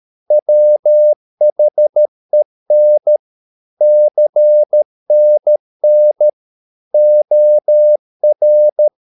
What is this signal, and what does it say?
Morse code, heard as a keyed tone, about 13 wpm, 600 Hz. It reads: WHEN CNN OR